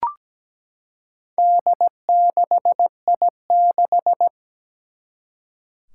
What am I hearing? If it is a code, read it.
Morse code, heard as a keyed tone, about 17 wpm, 700 Hz. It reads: D6I6
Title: Four Challenging Letter-Number Combinations 17wpm